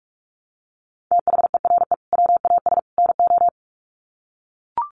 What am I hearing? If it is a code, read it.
Morse code, heard as a keyed tone, about 44 wpm, 700 Hz. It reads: THERE WAS NO
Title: Sets of 3 Words 45wpm